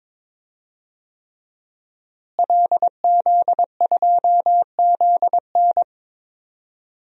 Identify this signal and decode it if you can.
Morse code, heard as a keyed tone, about 22 wpm, 700 Hz. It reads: LZ2ZN
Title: DXCC-100 Call Signs - 5 Characters - Repeated 3x - 22wpm